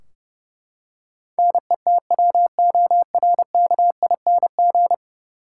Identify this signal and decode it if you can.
Morse code, heard as a keyed tone, about 30 wpm, 700 Hz. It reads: NETWORKING
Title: Practice QSO - Exchange 3 - More Info - 30wpm